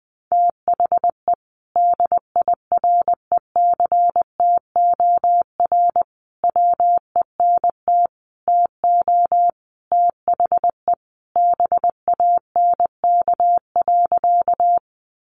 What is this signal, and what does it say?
Morse code, heard as a keyed tone, about 20 wpm, 700 Hz. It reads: THE DIRECTOR WENT TO THE BANK.
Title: Sentences from Top 1300 Words 20wpm